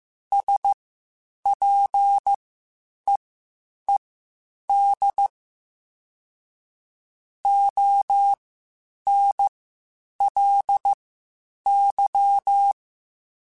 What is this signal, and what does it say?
Morse code, heard as a keyed tone, about 15 wpm, spaced as for 8 wpm, 800 Hz. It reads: SPEED ONLY
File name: Spaced Characters / Spaced Characters-mp3s / Weather5 1 sp @13